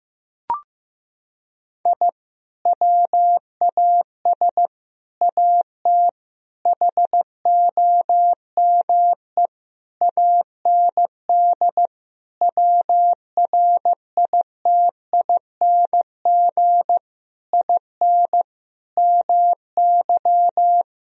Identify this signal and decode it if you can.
Morse code, heard as a keyed tone, about 15 wpm, 700 Hz. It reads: I WAS AT HOME AND WRITING IN MY
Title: Sentences from Top 1000 Words - Review Part 1 - 15wpm